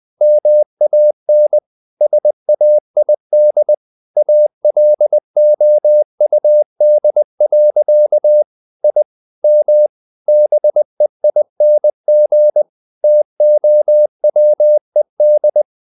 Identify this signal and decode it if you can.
Morse code, heard as a keyed tone, about 20 wpm, 600 Hz. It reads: MAN SAID ALOUD. I M BEING TOWED
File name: the old man and the sea / the old man and the sea part 1 20wpm